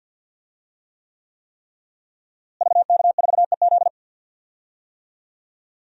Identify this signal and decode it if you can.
Morse code, heard as a keyed tone, about 52 wpm, 700 Hz. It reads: VK4EZ
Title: DXCC-100 Call Signs - 5 Characters - Repeated 3x - 50wpm